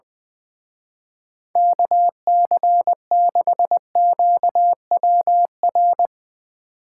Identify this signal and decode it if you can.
Morse code, heard as a keyed tone, about 21 wpm, 700 Hz. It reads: KC6QWR
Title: Speed Racing - USA 2x3 Call Signs 20wpm